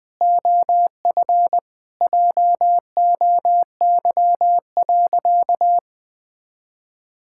Morse code, heard as a keyed tone, about 20 wpm, 700 Hz. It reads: OF JOY.
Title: Sentences with 3-Word Phrases - 20wpm